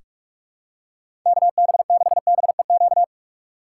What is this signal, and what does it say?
Morse code, heard as a keyed tone, about 43 wpm, 700 Hz. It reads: KB6BEQ
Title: USA Call Signs - 6 Characters - 3x Character Spacing - 45wpm